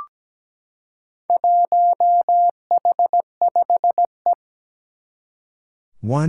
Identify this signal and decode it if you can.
Morse code, heard as a keyed tone, about 17 wpm, 700 Hz. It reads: 1H5E
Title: Four Challenging Letter-Number Combinations 17wpm